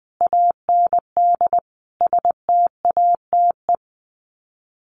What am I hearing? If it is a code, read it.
Morse code, heard as a keyed tone, about 20 wpm, 700 Hz. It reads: AND STATE
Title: Sets of 5 Words 20wpm